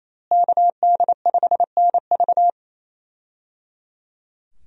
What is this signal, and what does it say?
Morse code, heard as a keyed tone, about 27 wpm, 700 Hz. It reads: KD5NV
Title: USA Call Signs - 5 Characters - Repeated 3x - 28wpm